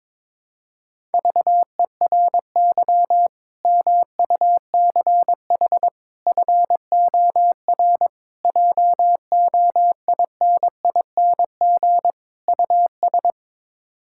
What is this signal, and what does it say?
Morse code, heard as a keyed tone, about 22 wpm, 700 Hz. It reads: VERY MUCH FOR JOINING US
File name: practice / 22wpm / Build A Phrase - 5-Word Phrases - 22wpm